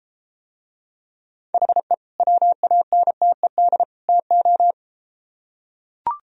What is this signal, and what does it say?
Morse code, heard as a keyed tone, about 30 wpm, 700 Hz. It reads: HE WANTED TO